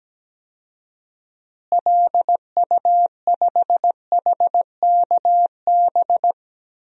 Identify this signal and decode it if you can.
Morse code, heard as a keyed tone, about 17 wpm, 700 Hz. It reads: LU5HKB
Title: DXCC-100 Call Signs - 6 Characters - Repeated 3x - 17wpm